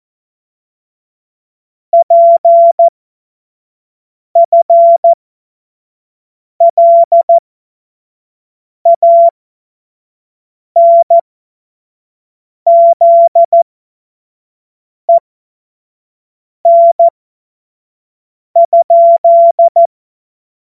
Morse code, heard as a keyed tone, about 14 wpm, spaced as for 5 wpm, 680 Hz. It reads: PFLANZEN?
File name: Morsekurs / Witze / Witz 13 Februar 2024 25 F70